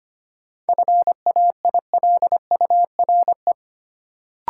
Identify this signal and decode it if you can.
Morse code, heard as a keyed tone, about 25 wpm, 700 Hz. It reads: FAILURE
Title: Sentences from Top 900 Words - New Word Drill - 25wpm